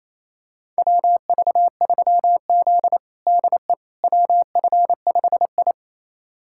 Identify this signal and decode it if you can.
Morse code, heard as a keyed tone, about 28 wpm, 700 Hz. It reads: WV3Z DE WF5I
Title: USA Callsign DE Callsign - 4 Characters - 28wpm